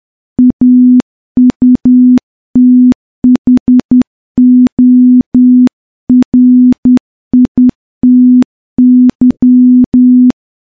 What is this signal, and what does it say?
Morse code, heard as a keyed tone, about 10 wpm, 250 Hz. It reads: AUTHORITY
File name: morse-training-10WPM